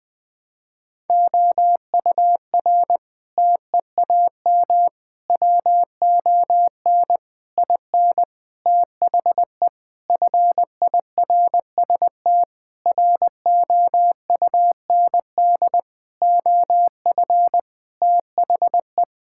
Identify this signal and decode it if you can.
Morse code, heard as a keyed tone, about 21 wpm, 700 Hz. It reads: OUR TEAM WON IN THE FIRST ROUND OF THE
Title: Sentences with 7-Word Phrases - Spoken Beginning - 20wpm